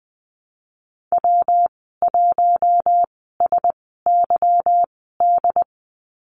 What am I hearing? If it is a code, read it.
Morse code, heard as a keyed tone, about 20 wpm, spaced as for 15 wpm, 700 Hz. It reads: W1SYD
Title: 179 Lesson 29g - Slash - Call Signs 20wpm